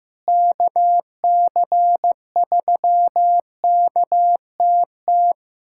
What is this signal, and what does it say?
Morse code, heard as a keyed tone, about 15 wpm, 700 Hz. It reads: KC3KTT
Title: USA Callsign DE Callsign 15wpm